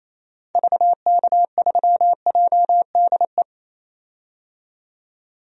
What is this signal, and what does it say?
Morse code, heard as a keyed tone, about 28 wpm, 700 Hz. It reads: VK3JDE
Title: DXCC-100 Call Signs - 6 Characters - 28wpm